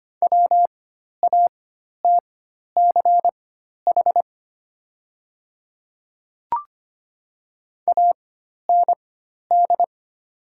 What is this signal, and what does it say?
Morse code, heard as a keyed tone, about 25 wpm, spaced as for 12 wpm, 690 Hz. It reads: WATCH AND
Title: 4x Character Spacing - Top 300 Words - 25wpm